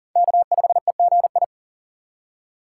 Morse code, heard as a keyed tone, about 38 wpm, 700 Hz. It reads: K5EGI
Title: USA Call Signs - 4x Character Spacing - 40wpm